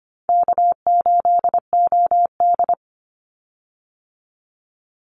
Morse code, heard as a keyed tone, about 25 wpm, 700 Hz. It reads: K8OD
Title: Call Signs Repeated 3x - 25wpm